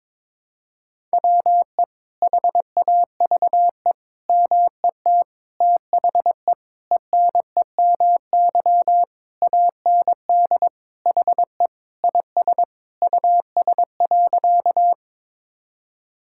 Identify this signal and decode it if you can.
Morse code, heard as a keyed tone, about 22 wpm, 700 Hz. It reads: WE HAVE MET THE ENEMY AND HE IS US.
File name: Sentences with 9-Word Phrases - 22wpm